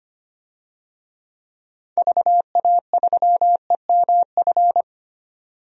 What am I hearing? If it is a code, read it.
Morse code, heard as a keyed tone, about 25 wpm, 700 Hz. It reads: VA3EMF
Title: DXCC-100 Call Signs - 6 Characters - 25wpm